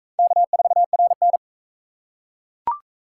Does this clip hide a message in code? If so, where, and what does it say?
Yes: Morse code from 0.2 to 1.4 s, heard as a keyed tone, about 41 wpm, 700 Hz. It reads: K4RN